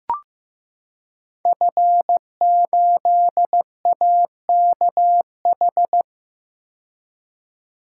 Morse code, heard as a keyed tone, about 15 wpm, 700 Hz. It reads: F8AKH
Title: Code Groups - 5 Letter:Numbers - 15wpm